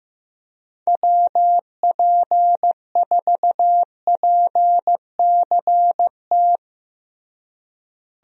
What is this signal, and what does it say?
Morse code, heard as a keyed tone, about 15 wpm, 700 Hz. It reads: WP4PCT
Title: USA Call Signs - 6 Characters - Repeated 3x - 15wpm